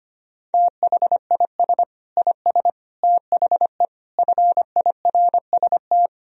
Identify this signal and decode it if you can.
Morse code, heard as a keyed tone, about 25 wpm, 700 Hz. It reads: THIS IS THE FIRST
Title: Sentences from Top 1000 Words - Review Part 1 - 25wpm